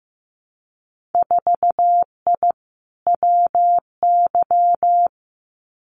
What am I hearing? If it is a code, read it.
Morse code, heard as a keyed tone, about 15 wpm, 700 Hz. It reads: 4I WY
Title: Winter Field Day Exchanges Encore 15wpm